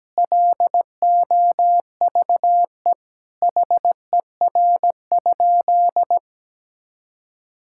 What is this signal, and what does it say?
Morse code, heard as a keyed tone, about 17 wpm, 700 Hz. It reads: LOVE HER?
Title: Sentences from Top 1600 Words 17wpm